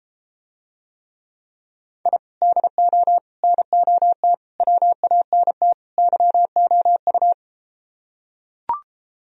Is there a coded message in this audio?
Yes: Morse code from 2.1 to 7.3 s, heard as a keyed tone, about 34 wpm, 700 Hz. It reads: I DO NOT WANT YOU